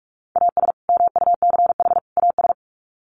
Morse code, heard as a keyed tone, about 44 wpm, 700 Hz. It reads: AS MUCH AS
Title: Sets of 3 Words 45wpm